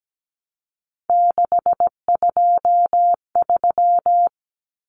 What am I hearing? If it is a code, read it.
Morse code, heard as a keyed tone, about 17 wpm, 700 Hz. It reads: 623
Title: Common Cut Numbers - 3-Digit 17wpm